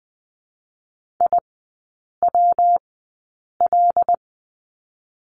Morse code, heard as a keyed tone, about 20 wpm, 700 Hz. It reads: I W L
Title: 114 Lesson 19e - Letter w - Three Characters 20wpm